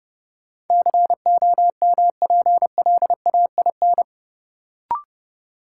Morse code, heard as a keyed tone, about 30 wpm, 700 Hz. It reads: COMPLAIN
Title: Sentences from Top 1300 Words - New Word Drill - 30wpm